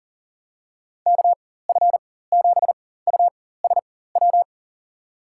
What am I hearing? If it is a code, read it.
Morse code, heard as a keyed tone, about 40 wpm, spaced as for 19 wpm, 700 Hz. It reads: KF7USW
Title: USA Call Signs - 6 Characters - 4x Character Spacing - 40wpm